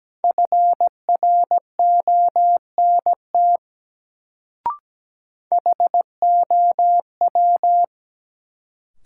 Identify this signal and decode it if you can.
Morse code, heard as a keyed tone, about 17 wpm, 700 Hz. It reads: FRONT HOW
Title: Sets of 2 Words - New Word Drill - 17wpm